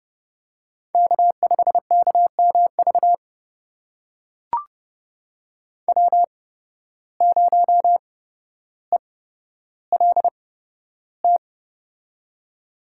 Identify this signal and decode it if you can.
Morse code, heard as a keyed tone, about 30 wpm, spaced as for 13 wpm, 700 Hz. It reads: K5KMV W0ELT
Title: USA Call Signs - 5 Characters - 8x Character Spacing - 30wpm